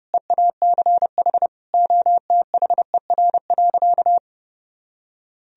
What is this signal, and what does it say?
Morse code, heard as a keyed tone, about 32 wpm, 700 Hz. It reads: EACH OTHER.
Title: Sentences from Top 1000 Words - Review Part 1 - 30wpm